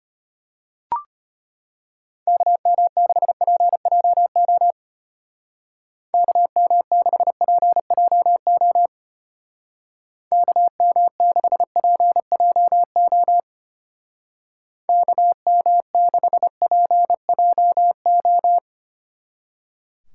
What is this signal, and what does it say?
Morse code, heard as a keyed tone, about 31 wpm, 700 Hz. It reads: KM6PJO KM6PJO KM6PJO KM6PJO
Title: Speed Racing - USA Call Signs 25wpm